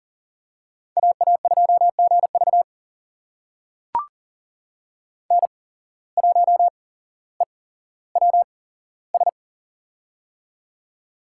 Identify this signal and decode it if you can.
Morse code, heard as a keyed tone, about 38 wpm, spaced as for 17 wpm, 700 Hz. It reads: AA2GV N1EWS
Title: USA Call Signs - 5 Characters - 8x Character Spacing - 40wpm